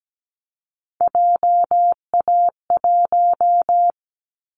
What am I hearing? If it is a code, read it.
Morse code, heard as a keyed tone, about 17 wpm, 700 Hz. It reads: JA1
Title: Callsign Prefixes for Top 50 DXCC - 17wpm